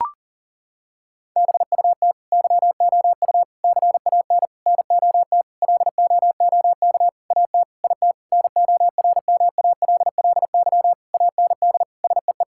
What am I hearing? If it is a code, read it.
Morse code, heard as a keyed tone, about 37 wpm, 700 Hz. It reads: BUT YOU CAN NOT LOOK AT IT NORMALLY AND SEE
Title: Sentences from Top 1000 Words - Review Part 6 - 40wpm